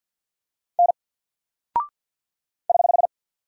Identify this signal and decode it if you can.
Morse code, heard as a keyed tone, about 50 wpm, 700 Hz. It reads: N <HH>
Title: Single Character (Letter:Number:Punctuation:Prosign) 50wpm